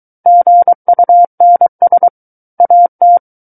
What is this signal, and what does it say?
Morse code, heard as a keyed tone, about 23 wpm, 700 Hz. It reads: GUNS AT
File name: Book One / 14-Chapter 14 - 23 WPM